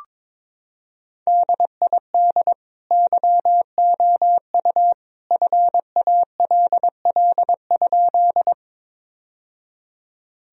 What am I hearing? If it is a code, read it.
Morse code, heard as a keyed tone, about 22 wpm, 700 Hz. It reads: DID YOU FALL?
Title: Sentences from Top 1000 Words - Review Part 2 - 22wpm